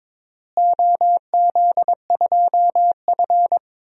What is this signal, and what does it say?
Morse code, heard as a keyed tone, about 22 wpm, 700 Hz. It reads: OZ2F
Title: DXCC-100 Call Signs - 4 Characters - Repeated 3x - 22wpm